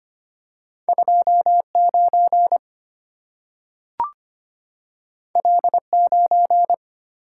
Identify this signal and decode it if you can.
Morse code, heard as a keyed tone, about 25 wpm, 700 Hz. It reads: 29 L9
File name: Two Letter-Number Combinations - 25wpm